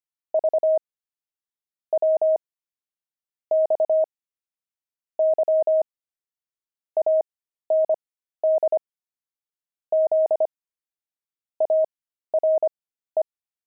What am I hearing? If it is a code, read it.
Morse code, heard as a keyed tone, about 25 wpm, spaced as for 13 wpm, 630 Hz. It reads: V W X Y AND Z ARE
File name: pangram-fs-13-cs-25-letters